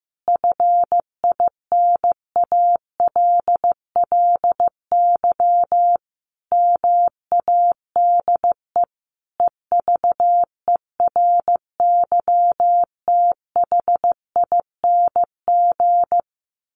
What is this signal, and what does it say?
Morse code, heard as a keyed tone, about 14 wpm, 700 Hz. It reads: FINALLY MADE EVERYTHING
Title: Sentences from Top 1300 Words 15wpm